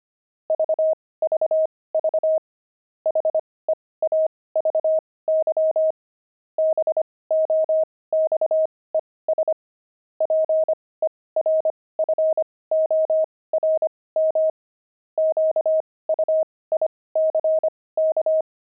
Morse code, heard as a keyed tone, about 25 wpm, spaced as for 18 wpm, 630 Hz. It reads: VVV HEAVY BOXES PERFORM QUICK
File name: pangram-fs-18-cs-25-heavy